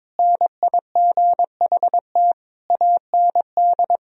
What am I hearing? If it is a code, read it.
Morse code, heard as a keyed tone, about 22 wpm, 700 Hz. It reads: NIGHT AND
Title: Sentences from Top 1000 Words - Review Part 4 - 22wpm